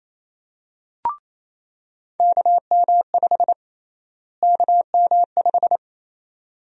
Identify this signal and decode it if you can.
Morse code, heard as a keyed tone, about 28 wpm, 700 Hz. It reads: KM5 KM5
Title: USA Call Sign Prefixes - Repeated 2x - 28wpm